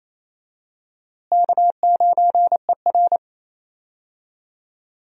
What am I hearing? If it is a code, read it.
Morse code, heard as a keyed tone, about 28 wpm, 700 Hz. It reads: K9ER
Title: USA Call Signs - 4 Characters - Repeated 3x - 28wpm